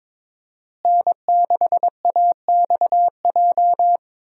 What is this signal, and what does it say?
Morse code, heard as a keyed tone, about 22 wpm, 700 Hz. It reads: N6AXJ
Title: USA Call Signs - 5 Characters - Repeated 2x - 22wpm